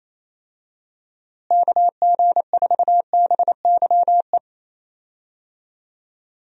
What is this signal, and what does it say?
Morse code, heard as a keyed tone, about 27 wpm, 700 Hz. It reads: KG4BYE
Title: USA Call Signs - 6 Characters - Repeated 3x - 28wpm